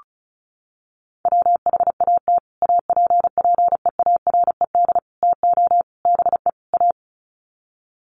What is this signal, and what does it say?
Morse code, heard as a keyed tone, about 34 wpm, 700 Hz. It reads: WHAT APPEARED TO BE A
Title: Sets of 5 Words Encore2 35wpm